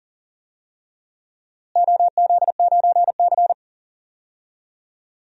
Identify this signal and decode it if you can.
Morse code, heard as a keyed tone, about 40 wpm, 700 Hz. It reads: OZ9C
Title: DXCC-100 Call Signs - 4 Characters - Repeated 2x - 40wpm